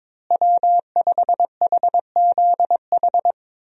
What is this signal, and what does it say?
Morse code, heard as a keyed tone, about 22 wpm, 700 Hz. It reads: W5HZH
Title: USA Call Signs - 5 Characters - Repeated 3x - 22wpm